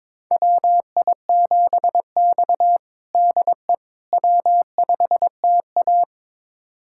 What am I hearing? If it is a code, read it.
Morse code, heard as a keyed tone, about 23 wpm, 700 Hz. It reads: WI7X DE W5TA
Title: USA Callsign DE Callsign - 4 Characters - 22wpm